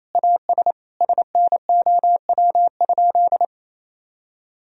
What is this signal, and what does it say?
Morse code, heard as a keyed tone, about 28 wpm, 700 Hz. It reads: AS SNOW?
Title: Sentences from Top 1000 Words - Review Part 3 - 28wpm